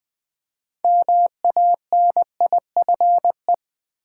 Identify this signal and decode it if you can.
Morse code, heard as a keyed tone, about 20 wpm, 700 Hz. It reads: MANIFE
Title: Build A Word - 2 Letters At A Time - 8-Letter Words - 20wpm